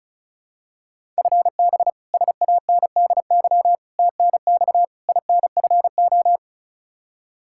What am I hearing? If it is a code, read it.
Morse code, heard as a keyed tone, about 34 wpm, 700 Hz. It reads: FB SANDY TNX INFO
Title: Practice QSO - Exchange 3 - More Info - 35wpm